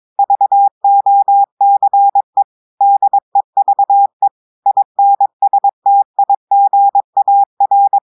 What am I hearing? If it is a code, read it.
Morse code, heard as a keyed tone, about 22 wpm, 800 Hz. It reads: VOCE DEVE INSTIGAR
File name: audios / portugais / portugais texte en clair 2 vitesse 22 mots minutes